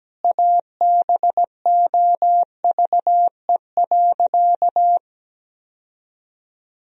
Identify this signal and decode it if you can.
Morse code, heard as a keyed tone, about 17 wpm, 700 Hz. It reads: ABOVE.
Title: Sentences from Top 1000 Words - Review Part 4 - 17wpm